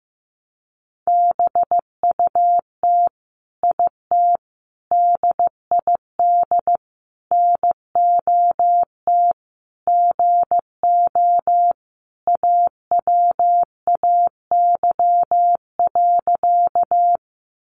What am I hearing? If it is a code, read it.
Morse code, heard as a keyed tone, about 15 wpm, 700 Hz. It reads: BUT IT DID NOT GO AWAY.
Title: Sentences from Top 600 Words 15wpm